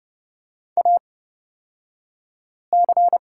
Morse code, heard as a keyed tone, about 30 wpm, spaced as for 5 wpm, 700 Hz. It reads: AC
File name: Lesson 38d - Letter q - Two Characters 30wpm